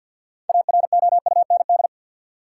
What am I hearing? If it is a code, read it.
Morse code, heard as a keyed tone, about 48 wpm, 700 Hz. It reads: AROUND
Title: Six Letter Words - 50wpm